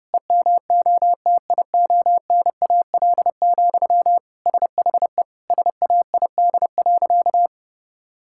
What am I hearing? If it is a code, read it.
Morse code, heard as a keyed tone, about 30 wpm, 700 Hz. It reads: EMOTIONAL, SHE SAID.
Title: Sentences from Top 1000 Words - Review Part 5 - 30wpm